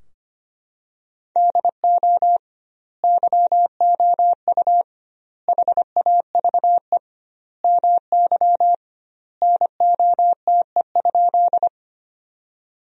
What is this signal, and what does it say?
Morse code, heard as a keyed tone, about 25 wpm, 700 Hz. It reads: DO YOU HAVE MY NOTE?
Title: 2x Word-Spacing - Sentences from Top 500 Words - 25wpm